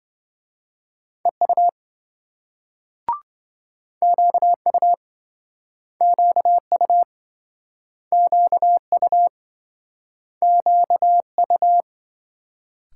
Morse code, heard as a keyed tone, about 26 wpm, 700 Hz. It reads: EU QU QU QU QU